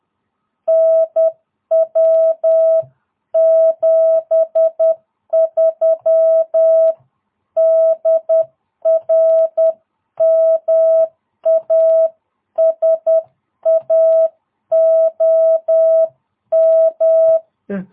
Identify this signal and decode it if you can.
Morse code, heard as a keyed tone, about 10 wpm, 650 Hz. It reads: NW 73 DR MASA OM
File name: jh8cbh / 2022cw